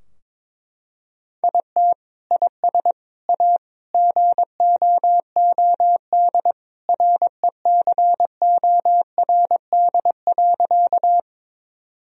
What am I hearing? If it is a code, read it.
Morse code, heard as a keyed tone, about 22 wpm, 700 Hz. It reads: IT IS A GOOD RECORD.
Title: Sentences from Top 1000 Words - Review Part 3 - 22wpm